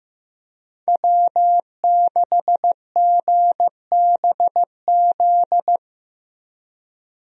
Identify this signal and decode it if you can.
Morse code, heard as a keyed tone, about 15 wpm, 700 Hz. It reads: W6GBZ
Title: USA Call Signs - 5 Characters - Repeated 2x - 15wpm